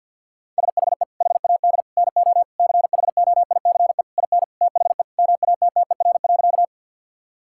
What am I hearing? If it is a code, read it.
Morse code, heard as a keyed tone, about 50 wpm, 700 Hz. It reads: SHE HAD NO CHOICE IN THE MATTER.
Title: Sentences from Top 1000 Words - Review Part 6 - 50wpm